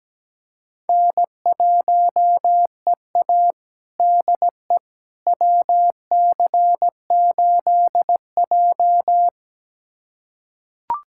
Morse code, heard as a keyed tone, about 17 wpm, 700 Hz. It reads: N1EA DE WC8J
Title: USA Callsign DE Callsign - 4 Characters - 17wpm